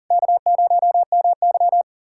Morse code, heard as a keyed tone, about 40 wpm, 700 Hz. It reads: K0MY
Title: USA Callsign DE Callsign - 4 Characters - 40wpm